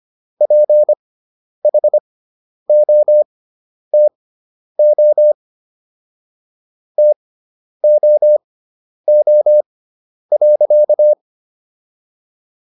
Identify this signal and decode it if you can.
Morse code, heard as a keyed tone, about 25 wpm, spaced as for 10 wpm, 600 Hz. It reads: PHOTO TOO.